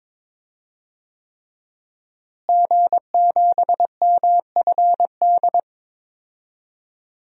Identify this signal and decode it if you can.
Morse code, heard as a keyed tone, about 21 wpm, 700 Hz. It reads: G7MFD
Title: DXCC-100 Call Signs - 5 Characters - Repeated 3x - 22wpm